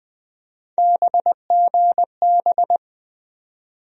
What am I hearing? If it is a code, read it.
Morse code, heard as a keyed tone, about 20 wpm, 700 Hz. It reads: BGB
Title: USA Call Sign Postfixes - 3 Letters 20wpm